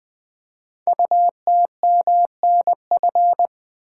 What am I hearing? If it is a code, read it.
Morse code, heard as a keyed tone, about 20 wpm, 700 Hz. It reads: UTMNF
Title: Code Groups - 5 Letters - 20wpm